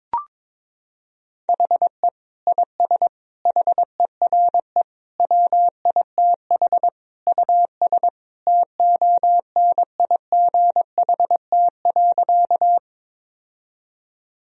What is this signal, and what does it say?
Morse code, heard as a keyed tone, about 21 wpm, 700 Hz. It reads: HE IS HERE WITH US TONIGHT.
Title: Sentences from Top 1000 Words - Review Part 3 - 22wpm